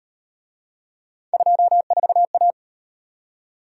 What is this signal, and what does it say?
Morse code, heard as a keyed tone, about 37 wpm, 700 Hz. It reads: 24A